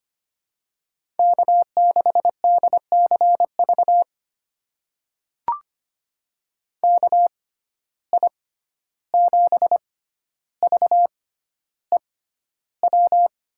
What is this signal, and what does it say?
Morse code, heard as a keyed tone, about 25 wpm, spaced as for 12 wpm, 700 Hz. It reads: K6DCV KI7VEW
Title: USA Call Signs - 6x Character Spacing - 25wpm